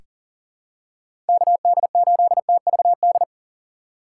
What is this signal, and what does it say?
Morse code, heard as a keyed tone, about 40 wpm, 690 Hz. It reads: KD8TVD
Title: USA Call Signs - 4x Character Spacing - 40wpm